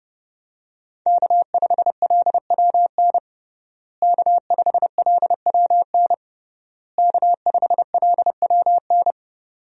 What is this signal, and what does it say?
Morse code, heard as a keyed tone, about 30 wpm, 700 Hz. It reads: K5LWN K5LWN K5LWN
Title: USA Call Signs - 5 Characters - Repeated 3x - 30wpm